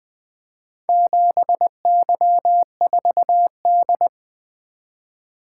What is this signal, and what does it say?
Morse code, heard as a keyed tone, about 23 wpm, 700 Hz. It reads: 7Y4D
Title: Speed-Racing - Four Letter-Number Combinations - 20wpm